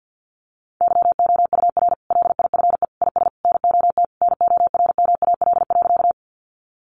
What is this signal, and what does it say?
Morse code, heard as a keyed tone, about 50 wpm, 700 Hz. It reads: YOUR LIFE IS NOT NORMAL.
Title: Sentences from Top 1200 Words 50wpm